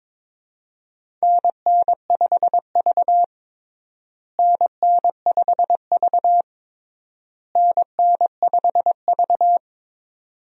Morse code, heard as a keyed tone, about 22 wpm, 700 Hz. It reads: NN5V NN5V NN5V
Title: USA Call Signs - 4 Characters - Repeated 3x - 22wpm